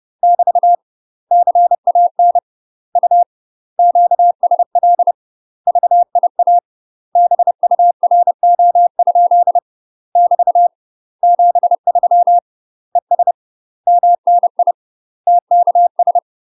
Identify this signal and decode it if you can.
Morse code, heard as a keyed tone, about 30 wpm, 700 Hz. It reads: <BT> CAN U QSL VIA BURO? <BT> 73 ES MNI TKS
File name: UK MASTER QSOs TRAINING / 30WPM / UK MASTER QSOs 0036